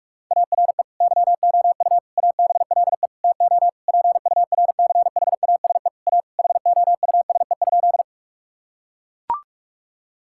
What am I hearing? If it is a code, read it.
Morse code, heard as a keyed tone, about 41 wpm, 700 Hz. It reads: ARE YOU ABLE TO PURCHASE A HOUSE?
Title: Sentences from Top 1000 Words - Review Part 5 - 45wpm